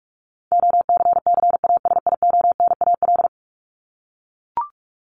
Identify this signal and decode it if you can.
Morse code, heard as a keyed tone, about 44 wpm, 700 Hz. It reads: OCCASIONAL
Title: LY Suffix - Top 100 Words with Base Words 45wpm